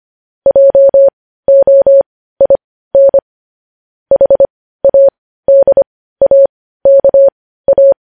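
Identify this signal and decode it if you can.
Morse code, heard as a keyed tone, about 25 wpm, spaced as for 15 wpm, 550 Hz. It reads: JOIN HADAKA
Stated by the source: Morse code news headlines sent at 15.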